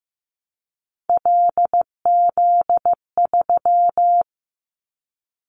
Morse code, heard as a keyed tone, about 15 wpm, 700 Hz. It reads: LZ3
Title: Callsign Prefixes for Top 100 DXCC - 15wpm